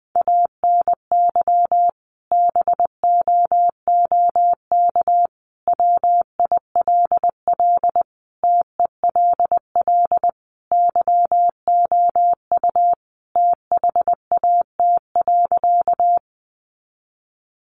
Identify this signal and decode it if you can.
Morse code, heard as a keyed tone, about 20 wpm, 700 Hz. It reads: ANY BOOK WILL TELL YOU THAT.
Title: Sentences from Top 300 Words 20wpm